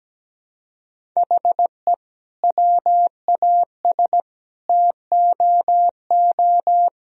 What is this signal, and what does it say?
Morse code, heard as a keyed tone, about 17 wpm, 700 Hz. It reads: HE WAS TOO
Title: Sentences from Top 1000 Words - Review Part 2 - 17wpm